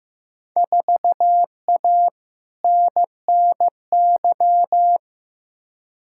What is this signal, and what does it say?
Morse code, heard as a keyed tone, about 15 wpm, 700 Hz. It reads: 4A NNY
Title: Field Day Exchanges - 15wpm